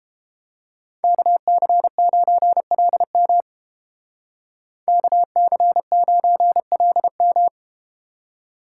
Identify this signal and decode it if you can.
Morse code, heard as a keyed tone, about 32 wpm, 700 Hz. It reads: KC9LM KC9LM